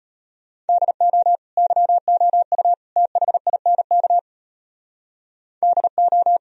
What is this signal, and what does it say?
Morse code, heard as a keyed tone, about 32 wpm, 700 Hz. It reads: DO YOU THINK DO